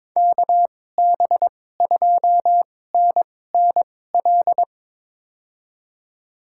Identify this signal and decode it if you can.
Morse code, heard as a keyed tone, about 22 wpm, spaced as for 16 wpm, 700 Hz. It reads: KB2NNL
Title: USA Call Signs - 6 Characters - 2x Character Spacing - 22wpm